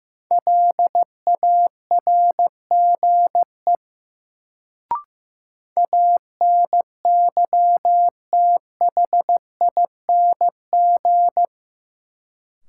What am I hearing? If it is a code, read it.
Morse code, heard as a keyed tone, about 15 wpm, 700 Hz. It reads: LARGE ANYTHING
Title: Sets of 4 Words - New Word Drill - 15wpm